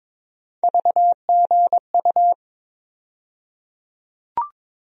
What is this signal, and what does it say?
Morse code, heard as a keyed tone, about 22 wpm, 700 Hz. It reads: VGU
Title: USA Call Sign Postfixes - 3 Letters 22wpm